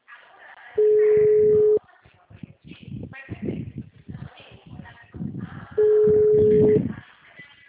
Ringback tone (Europe, 425 Hz).